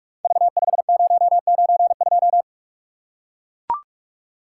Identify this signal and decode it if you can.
Morse code, heard as a keyed tone, about 44 wpm, 700 Hz. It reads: V5092
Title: Code Groups - 5 Letter:Numbers - 45wpm